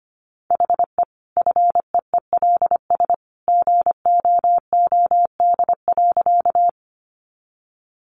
Morse code, heard as a keyed tone, about 25 wpm, 700 Hz. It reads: HE FEELS GOOD.
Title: Additional Easy Phrases 25wpm